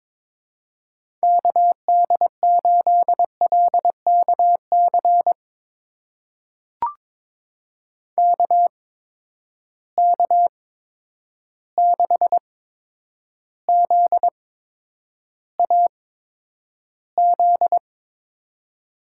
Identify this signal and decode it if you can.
Morse code, heard as a keyed tone, about 23 wpm, spaced as for 10 wpm, 700 Hz. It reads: KD8LKC KK6ZAZ
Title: USA Call Signs - 6 Characters - 8x Character Spacing - 22wpm